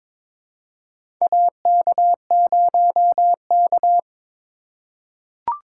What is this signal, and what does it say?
Morse code, heard as a keyed tone, about 22 wpm, 700 Hz. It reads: AK0K
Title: USA Call Signs - 4 Characters - 3x Character Spacing - 22wpm